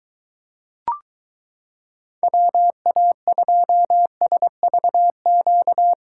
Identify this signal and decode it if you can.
Morse code, heard as a keyed tone, about 23 wpm, 700 Hz. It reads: WA2SVQ